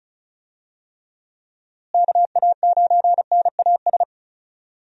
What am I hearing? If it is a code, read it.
Morse code, heard as a keyed tone, about 34 wpm, 700 Hz. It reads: KA9NAS
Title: USA Call Signs - 6 Characters - Repeated 2x - 35wpm